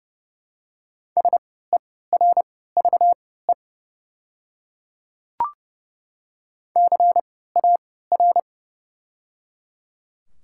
Morse code, heard as a keyed tone, about 30 wpm, spaced as for 17 wpm, 700 Hz. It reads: SERVE CAR
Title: 3x Character Spacing - Top 500 Words - 30wpm